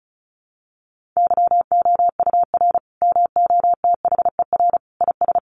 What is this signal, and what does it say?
Morse code, heard as a keyed tone, about 34 wpm, 700 Hz. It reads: YOUR MOTHER IS
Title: Sentences from Top 1400 Words 35wpm